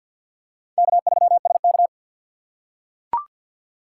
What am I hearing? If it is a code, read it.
Morse code, heard as a keyed tone, about 45 wpm, 700 Hz. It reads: K3SK
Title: USA Call Signs - 4 Characters - 2x Character Spacing - 50wpm